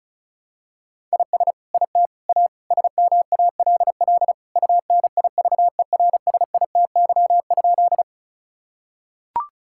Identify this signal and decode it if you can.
Morse code, heard as a keyed tone, about 34 wpm, 700 Hz. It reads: IS IT A SMALL UNIVERSITY?
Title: Sentences from Top 1000 Words - Review Part 5 - 35wpm